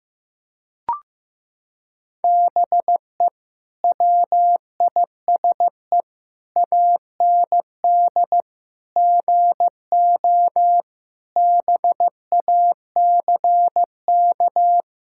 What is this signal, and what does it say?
Morse code, heard as a keyed tone, about 15 wpm, 700 Hz. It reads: BE WISE AND GO BACK
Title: Sentences from Top 1600 Words 15wpm